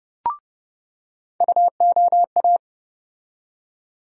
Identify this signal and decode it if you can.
Morse code, heard as a keyed tone, about 31 wpm, 700 Hz. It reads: UOA
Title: USA Call Sign Postfixes - 3 Letters 30wpm